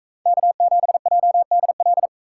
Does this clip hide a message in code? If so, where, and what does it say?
Yes: Morse code from 0.3 to 2.1 s, heard as a keyed tone, about 42 wpm, 700 Hz. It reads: K7JDL